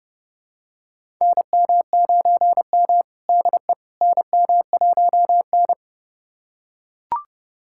Morse code, heard as a keyed tone, about 30 wpm, 700 Hz. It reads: NM9M DE NM1N
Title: USA Callsign DE Callsign - 4 Characters - 30wpm